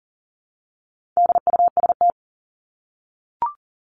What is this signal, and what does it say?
Morse code, heard as a keyed tone, about 40 wpm, 700 Hz. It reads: DUST
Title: Top 1751-2000 Words 40wpm